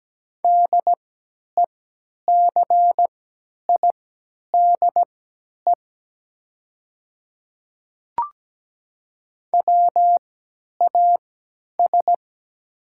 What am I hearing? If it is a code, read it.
Morse code, heard as a keyed tone, about 17 wpm, spaced as for 10 wpm, 700 Hz. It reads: DECIDE WAS
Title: 3x Character Spacing - Top 500 Words - 17wpm